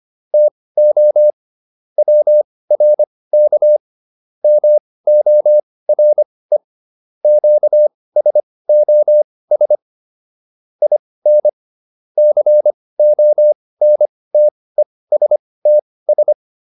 Morse code, heard as a keyed tone, about 25 wpm, spaced as for 18 wpm, 600 Hz. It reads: TO WRK MORE QSOS IN CONTESTS